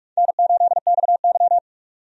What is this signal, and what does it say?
Morse code, heard as a keyed tone, about 45 wpm, 700 Hz. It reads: N8XY
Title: USA Call Signs - 4 Characters - 45wpm